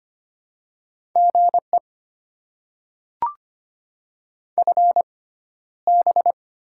Morse code, heard as a keyed tone, about 25 wpm, spaced as for 13 wpm, 690 Hz. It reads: GE FB
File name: Top 20 QSO Elements - 6x Character Spacing - 25wpm